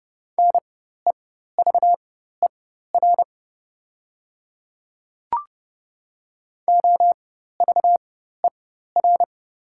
Morse code, heard as a keyed tone, about 30 wpm, spaced as for 14 wpm, 690 Hz. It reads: NEVER OVER
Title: 4x Character Spacing - Top 300 Words - 30wpm